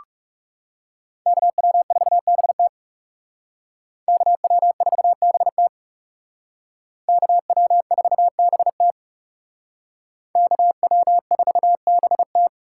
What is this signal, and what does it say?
Morse code, heard as a keyed tone, about 36 wpm, 700 Hz. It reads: KW4BT KW4BT KW4BT KW4BT
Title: Speed Racing - USA 2x2 Call Signs 30wpm